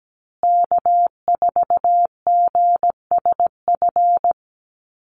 Morse code, heard as a keyed tone, about 17 wpm, 700 Hz. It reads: K4GSF
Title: USA Call Signs Repeated 2x 17wpm